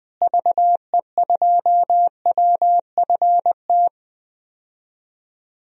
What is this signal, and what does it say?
Morse code, heard as a keyed tone, about 20 wpm, 700 Hz. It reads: VE2WFT
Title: DXCC-100 Call Signs - 6 Characters - Repeated 3x - 20wpm